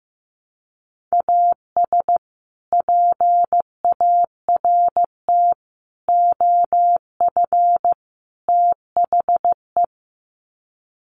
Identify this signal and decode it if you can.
Morse code, heard as a keyed tone, about 15 wpm, 700 Hz. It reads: AS PART OF THE
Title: Sets of 4 Words 15wpm